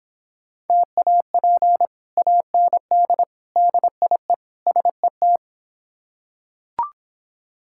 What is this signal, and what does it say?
Morse code, heard as a keyed tone, about 25 wpm, 700 Hz. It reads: TAP AND DIE SET
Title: Difficult Words - Speed-Racing with Hints - 20wpm